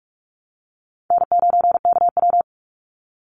Morse code, heard as a keyed tone, about 46 wpm, 690 Hz. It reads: N9KW
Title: NA SOTA Chaser Callsigns 45wpm